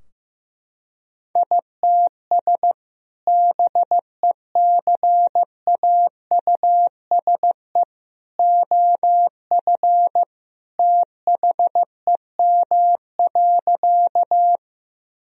Morse code, heard as a keyed tone, about 15 wpm, 700 Hz. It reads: ITS BECAUSE OF THEM.
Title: Sentences from Top 1000 Words - Review Part 3 - 15wpm